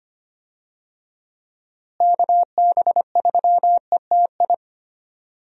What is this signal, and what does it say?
Morse code, heard as a keyed tone, about 25 wpm, 700 Hz. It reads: KB3ETI
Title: USA Call Signs - 6 Characters - Repeated 3x - 25wpm